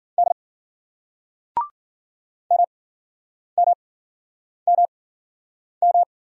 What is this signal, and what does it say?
Morse code, heard as a keyed tone, about 49 wpm, spaced as for 9 wpm, 700 Hz. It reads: D MMMM
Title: Speed-Racing Single Letters 40wpm